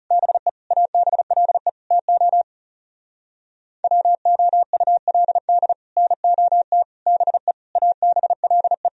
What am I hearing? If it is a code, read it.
Morse code, heard as a keyed tone, about 39 wpm, 700 Hz. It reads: BE ABLE TO WOULD NOT BE ABLE